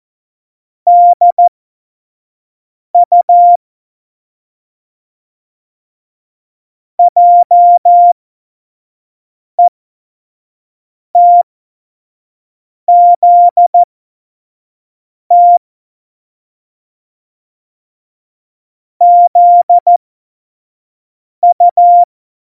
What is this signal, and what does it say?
Morse code, heard as a keyed tone, about 14 wpm, spaced as for 5 wpm, 700 Hz. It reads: DU JETZT ZU
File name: Morsekurs / Witze / Witz 07. Januar 2020, 25 F70